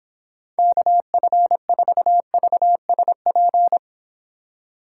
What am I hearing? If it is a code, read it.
Morse code, heard as a keyed tone, about 26 wpm, 700 Hz. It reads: KF4VSP